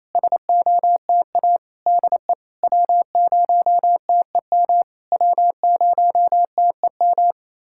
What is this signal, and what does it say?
Morse code, heard as a keyed tone, about 28 wpm, 700 Hz. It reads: SOTA DE W0TEM W0TEM
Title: SOTA QSO Practice 28wpm